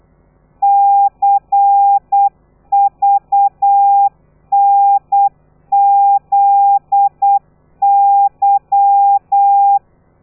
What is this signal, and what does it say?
Morse code, heard as a keyed tone, about 8 wpm, 800 Hz. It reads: CVNZY
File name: telegrafia / i2tfj / Lez13 35